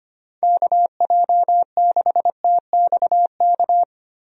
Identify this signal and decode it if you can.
Morse code, heard as a keyed tone, about 25 wpm, 700 Hz. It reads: KJ6TXK
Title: USA Call Signs - 6 Characters - 25wpm